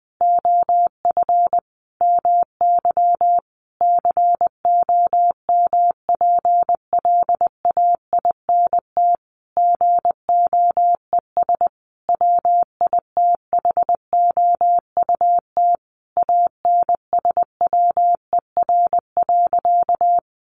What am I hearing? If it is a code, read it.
Morse code, heard as a keyed tone, about 20 wpm, 700 Hz. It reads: OF MY COMPLAINT GOES WITHOUT ANSWER.
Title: Sentences from Top 1500 Words 20wpm